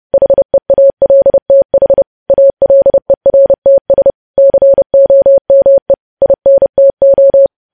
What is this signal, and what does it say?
Morse code, heard as a keyed tone, about 30 wpm, 550 Hz. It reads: HEALTH ALERTS COME INTO
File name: podcast / News headlines-1x-30x30